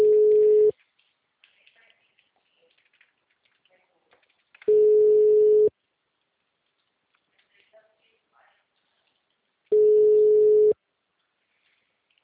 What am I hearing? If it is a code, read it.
Ringback tone (Europe, 425 Hz).